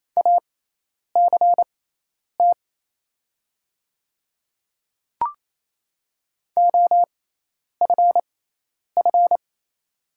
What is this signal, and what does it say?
Morse code, heard as a keyed tone, about 28 wpm, spaced as for 10 wpm, 700 Hz. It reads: ACT OFF
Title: 6x Character Spacing - Top 300 Words - 28wpm